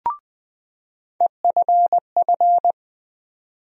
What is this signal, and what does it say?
Morse code, heard as a keyed tone, about 20 wpm, 700 Hz. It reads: EFF
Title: USA Call Sign Postfixes - 3 Letters 20wpm